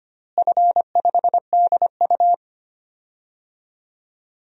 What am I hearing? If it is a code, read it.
Morse code, heard as a keyed tone, about 25 wpm, 700 Hz. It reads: F5DU
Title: DXCC-100 Call Signs - 4 Characters - Repeated 2x - 25wpm